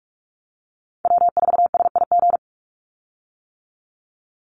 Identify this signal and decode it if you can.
Morse code, heard as a keyed tone, about 44 wpm, 700 Hz. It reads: W4SIG
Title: Call Signs - 45wpm